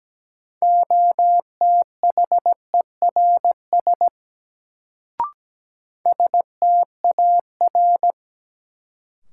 Morse code, heard as a keyed tone, about 17 wpm, 700 Hz. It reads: OTHERS STAR
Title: Sentences from Top 1000 Words - Review Part 2 - New Word Drill - 17wpm